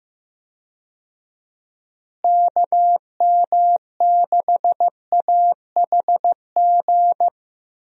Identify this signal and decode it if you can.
Morse code, heard as a keyed tone, about 15 wpm, 700 Hz. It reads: KM6AHG
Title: USA Call Signs - 6 Characters - Repeated 2x - 15wpm